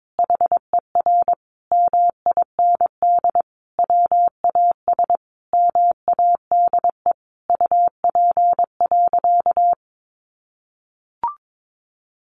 Morse code, heard as a keyed tone, about 22 wpm, 700 Hz. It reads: HER MIND WAS MADE UP.
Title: Sentences from Top 1300 Words 22wpm